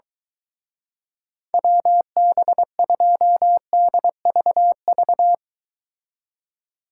Morse code, heard as a keyed tone, about 23 wpm, 700 Hz. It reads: WB2DVV